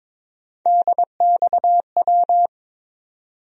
Morse code, heard as a keyed tone, about 22 wpm, 700 Hz. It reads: DXW
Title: USA Call Sign Postfixes - 3 Letters 22wpm